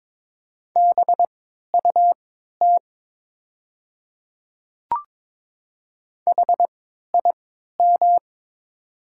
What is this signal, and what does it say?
Morse code, heard as a keyed tone, about 22 wpm, spaced as for 12 wpm, 700 Hz. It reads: BUT HIM
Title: 3x Character Spacing - Top 100 Words - 22wpm